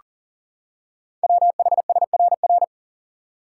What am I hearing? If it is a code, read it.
Morse code, heard as a keyed tone, about 40 wpm, 700 Hz. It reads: WHSRR
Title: Code Groups - 5 Letters - 40wpm